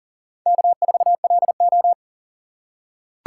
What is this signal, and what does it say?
Morse code, heard as a keyed tone, about 40 wpm, 700 Hz. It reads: K4LO
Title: USA Call Signs - 4 Characters - 2x Character Spacing - 40wpm